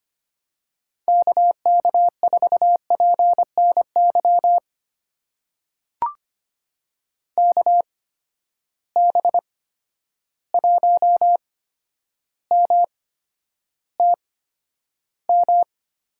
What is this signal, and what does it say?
Morse code, heard as a keyed tone, about 25 wpm, spaced as for 11 wpm, 700 Hz. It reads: KK4PNY KB1MTM
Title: USA Call Signs - 6 Characters - 8x Character Spacing - 25wpm